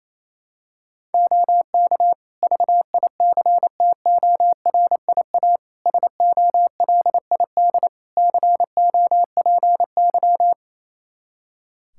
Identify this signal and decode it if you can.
Morse code, heard as a keyed tone, about 27 wpm, 700 Hz. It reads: OK VICTORIA SOLID COPY
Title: Practice QSO - Exchange 3 - More Info - 28wpm